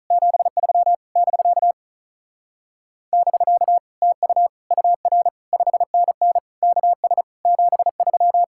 Morse code, heard as a keyed tone, about 43 wpm, 700 Hz. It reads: 73 <BK> <BK> TU UR 5NN KS 73
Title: Speed-Racing Common QSO Phrases Encore - 35wpm